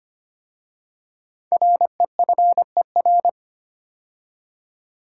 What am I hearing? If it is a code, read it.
Morse code, heard as a keyed tone, about 25 wpm, 700 Hz. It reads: REFER
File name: 5 Letter Words - First 3 Letters Given - 25wpm